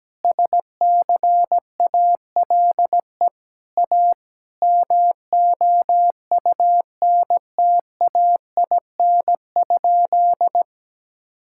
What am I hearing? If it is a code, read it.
Morse code, heard as a keyed tone, about 17 wpm, 700 Hz. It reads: SCALE A MOUNTAIN?
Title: Sentences from Top 1000 Words - Review Part 5 - 17wpm